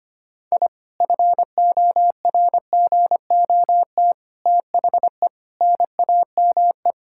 Morse code, heard as a keyed tone, about 25 wpm, 700 Hz. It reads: I FORGOT THE NAME
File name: practice / 25wpm / Sentences with 4-Word Phrases - 25wpm